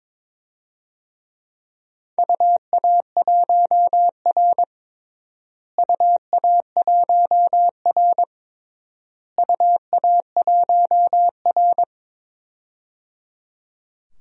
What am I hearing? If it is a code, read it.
Morse code, heard as a keyed tone, about 22 wpm, 700 Hz. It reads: UA1R UA1R UA1R
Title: DXCC-100 Call Signs - 4 Characters - Repeated 3x - 22wpm